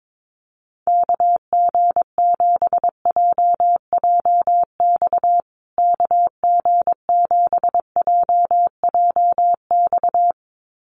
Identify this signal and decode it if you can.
Morse code, heard as a keyed tone, about 22 wpm, 700 Hz. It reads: KG7JJX KG7JJX
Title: USA Call Signs Repeated 2x 22wpm